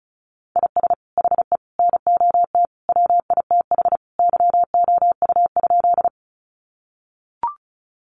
Morse code, heard as a keyed tone, about 34 wpm, 700 Hz. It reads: IS HE NOT WITH YOU?
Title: Sentences from Top 600 Words 35wpm